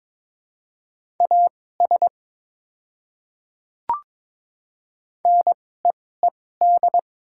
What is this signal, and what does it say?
Morse code, heard as a keyed tone, about 22 wpm, spaced as for 16 wpm, 700 Hz. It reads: AS NEED
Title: 2x Character Spacing - Top 200 Words - 22wpm